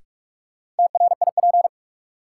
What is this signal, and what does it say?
Morse code, heard as a keyed tone, about 46 wpm, 700 Hz. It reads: TRIP
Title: Sentences from Top 1000 Words - Review Part 4 - New Word Drill - 45wpm